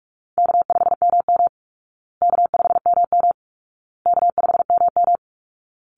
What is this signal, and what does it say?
Morse code, heard as a keyed tone, about 44 wpm, 700 Hz. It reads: K5MM K5MM K5MM
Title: Call Signs Repeated 2x - 45wpm